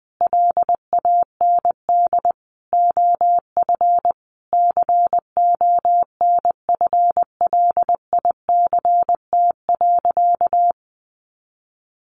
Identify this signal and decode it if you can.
Morse code, heard as a keyed tone, about 20 wpm, 700 Hz. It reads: LAND OF CONFLICT.
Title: Sentences from Top 1300 Words 20wpm